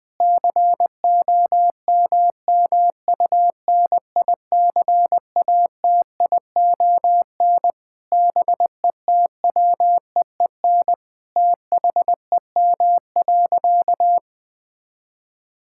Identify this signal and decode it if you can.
Morse code, heard as a keyed tone, about 20 wpm, 700 Hz. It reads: COMMUNICATION BETWEEN THEM.
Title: Sentences from Top 1000 Words - Review Part 6 - 20wpm